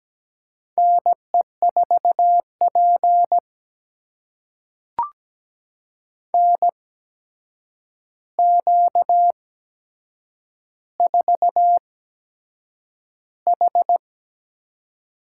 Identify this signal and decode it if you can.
Morse code, heard as a keyed tone, about 17 wpm, spaced as for 7 wpm, 700 Hz. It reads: NE4P NQ4H
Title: USA Call Signs - 4 Characters - 8x Character Spacing - 17wpm